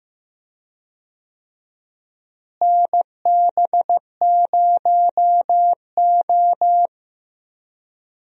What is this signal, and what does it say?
Morse code, heard as a keyed tone, about 15 wpm, 700 Hz. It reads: NB0O
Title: USA Call Signs - 4 Characters - Repeated 2x - 15wpm